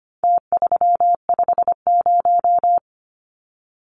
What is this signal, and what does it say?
Morse code, heard as a keyed tone, about 25 wpm, 700 Hz. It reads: T350
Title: Numbers - Contest Style - 4-Digit 25wpm